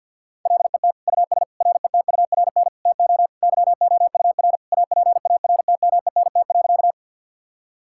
Morse code, heard as a keyed tone, about 50 wpm, 700 Hz. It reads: LET US RETURN TO YOUR APARTMENT.
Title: Sentences from Top 1000 Words - Review Part 6 - 50wpm